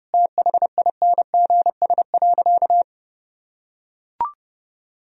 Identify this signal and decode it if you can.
Morse code, heard as a keyed tone, about 32 wpm, 700 Hz. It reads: THINGS.
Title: Sentences from Top 1000 Words - Review Part 4 - 30wpm